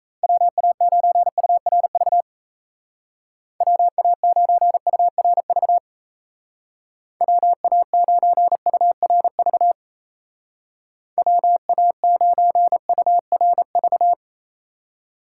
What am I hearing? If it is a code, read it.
Morse code, heard as a keyed tone, about 34 wpm, 700 Hz. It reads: WA9URV WA9URV WA9URV WA9URV
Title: Speed Racing - USA 2x3 Call Signs 28wpm